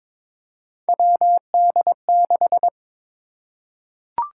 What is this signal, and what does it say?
Morse code, heard as a keyed tone, about 22 wpm, 700 Hz. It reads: WD6
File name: practice / 22wpm / Three Letter-Number Combinations - 22wpm